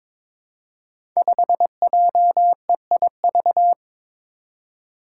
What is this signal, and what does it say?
Morse code, heard as a keyed tone, about 22 wpm, 700 Hz. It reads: 5JEIV
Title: Five Challenging Letter-Number Combinations 22wpm